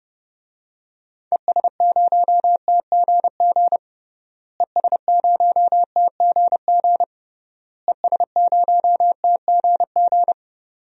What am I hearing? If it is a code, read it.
Morse code, heard as a keyed tone, about 32 wpm, 700 Hz. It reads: ES0TGG ES0TGG ES0TGG
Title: DXCC-100 Call Signs - 6 Characters - Repeated 3x - 30wpm